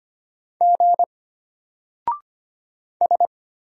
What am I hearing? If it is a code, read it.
Morse code, heard as a keyed tone, about 25 wpm, 700 Hz. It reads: G S
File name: Single-Letter-Number-25wpm